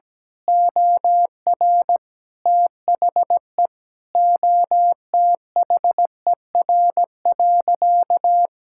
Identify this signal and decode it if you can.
Morse code, heard as a keyed tone, about 17 wpm, 700 Hz. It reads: OR THE OTHER.
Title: Sentences from Top 1000 Words - Review Part 6 - 17wpm